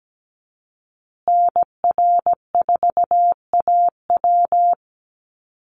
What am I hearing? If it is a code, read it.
Morse code, heard as a keyed tone, about 17 wpm, 700 Hz. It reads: NR4AW
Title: USA Call Signs Repeated 3x 17wpm